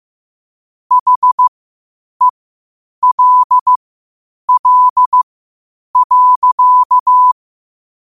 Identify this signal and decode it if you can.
Morse code, heard as a keyed tone, about 15 wpm, spaced as for 8 wpm, 1000 Hz. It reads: HELL.
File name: qotdfiles / 8wpm